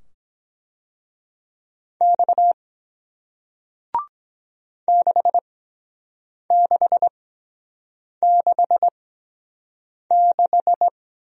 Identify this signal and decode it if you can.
Morse code, heard as a keyed tone, about 23 wpm, spaced as for 7 wpm, 700 Hz. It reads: X 6666